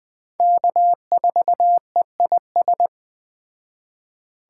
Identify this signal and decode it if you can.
Morse code, heard as a keyed tone, about 22 wpm, 700 Hz. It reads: K4EIS